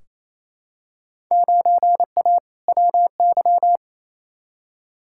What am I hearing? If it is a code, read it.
Morse code, heard as a keyed tone, about 28 wpm, 700 Hz. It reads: 9A WY
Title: Field Day Exchanges Encore - 28wpm